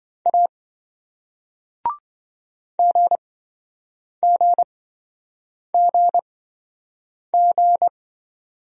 Morse code, heard as a keyed tone, about 27 wpm, spaced as for 7 wpm, 700 Hz. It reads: A GGGG